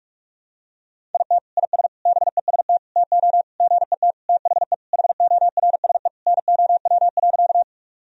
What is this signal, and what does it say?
Morse code, heard as a keyed tone, about 41 wpm, 700 Hz. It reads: IT IS BEST TO GET THE HORSE NOW.
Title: Sentences from Top 1000 Words - Review Part 1 - 45wpm